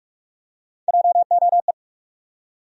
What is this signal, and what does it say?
Morse code, heard as a keyed tone, about 43 wpm, 700 Hz. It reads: JOE
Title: USA Common Names - Max 4 Letters - 45wpm